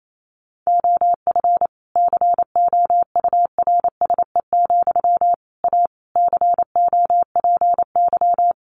Morse code, heard as a keyed tone, about 28 wpm, 700 Hz. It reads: OF COURSE, A COPY
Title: Sentences from Top 1400 Words 28wpm